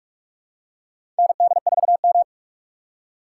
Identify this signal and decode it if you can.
Morse code, heard as a keyed tone, about 46 wpm, 700 Hz. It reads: ND4M
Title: USA Call Signs - 6x Character Spacing - 45wpm